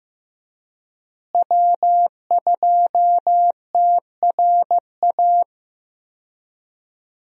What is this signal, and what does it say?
Morse code, heard as a keyed tone, about 15 wpm, 700 Hz. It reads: W2TRA
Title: USA Call Signs - 5 Characters - Repeated 3x - 15wpm